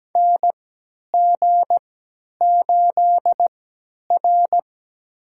Morse code, heard as a keyed tone, about 17 wpm, spaced as for 10 wpm, 700 Hz. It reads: NG8R
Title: USA Call Signs - 4 Characters - 3x Character Spacing - 17wpm